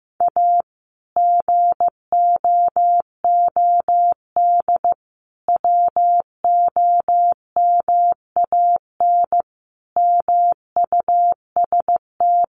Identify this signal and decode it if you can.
Morse code, heard as a keyed tone, about 15 wpm, 700 Hz. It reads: A GOOD WOMAN MUST
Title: Sentences from Top 1100 Words 15wpm